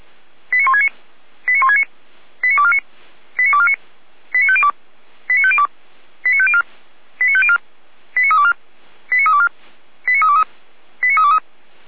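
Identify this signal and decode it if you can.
Five-tone selective call (ZVEI), heard as quick runs of five tones, tones of about 80 ms. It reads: 89169 89169 89249 89249 89502 89502 89504 89504 89324 89324 89323 89323
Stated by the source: ZVEI five-tone call received by SDR, from the Feuerwehr Orlamünde fire service.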